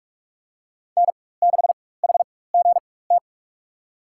Morse code, heard as a keyed tone, about 46 wpm, spaced as for 21 wpm, 700 Hz. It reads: N6HGT
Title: USA Call Signs - 5 Characters - 4x Character Spacing - 45wpm